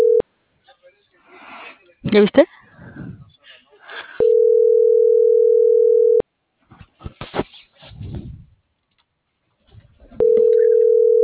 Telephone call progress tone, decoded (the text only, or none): ringback tone